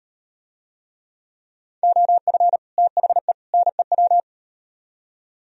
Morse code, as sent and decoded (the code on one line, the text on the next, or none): --- ..-. / - .... / -. .--
OF THE NEW